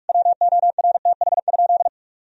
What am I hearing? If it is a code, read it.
Morse code, heard as a keyed tone, about 46 wpm, 700 Hz. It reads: WORTH?
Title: Sentences from Top 1000 Words - Review Part 2 - 45wpm